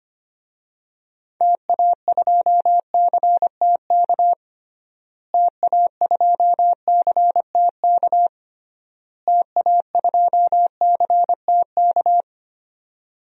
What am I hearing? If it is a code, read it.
Morse code, heard as a keyed tone, about 25 wpm, 700 Hz. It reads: TA2CTK TA2CTK TA2CTK
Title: DXCC-100 Call Signs - 6 Characters - Repeated 3x - 25wpm